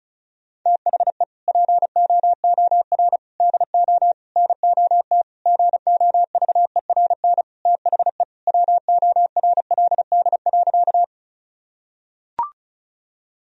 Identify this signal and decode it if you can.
Morse code, heard as a keyed tone, about 36 wpm, 700 Hz. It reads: THE POOR DO NOT GOVERN THE WORLD.
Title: Sentences from Top 1000 Words - Review Part 2 - 35wpm